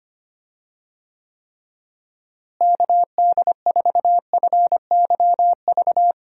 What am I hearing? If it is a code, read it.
Morse code, heard as a keyed tone, about 25 wpm, 700 Hz. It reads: KD4FYV
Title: USA Call Signs - 6 Characters - Repeated 3x - 25wpm